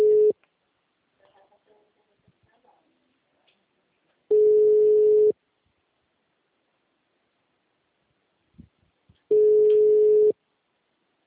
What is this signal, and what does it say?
Ringback tone (Europe, 425 Hz).
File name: out- -2014- - - .90627